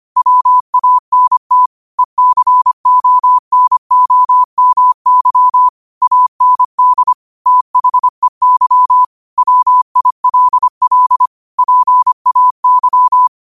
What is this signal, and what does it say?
Morse code, heard as a keyed tone, about 25 wpm, 1000 Hz. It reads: WANT ECONOMY AND THEY WILL PAY